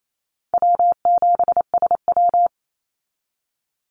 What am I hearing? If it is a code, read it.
Morse code, heard as a keyed tone, about 28 wpm, 700 Hz. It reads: W7SW
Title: Easy Call Signs Repeated 2x - 28wpm